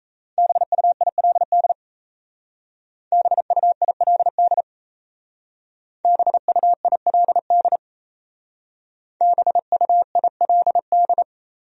Morse code, heard as a keyed tone, about 36 wpm, 700 Hz. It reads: BUILD BUILD BUILD BUILD